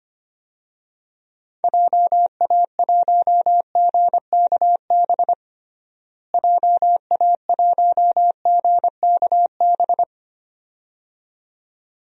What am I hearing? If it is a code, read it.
Morse code, heard as a keyed tone, about 25 wpm, 700 Hz. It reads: JA1GKB JA1GKB
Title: DXCC-100 Call Signs - 6 Characters - Repeated 2x - 25wpm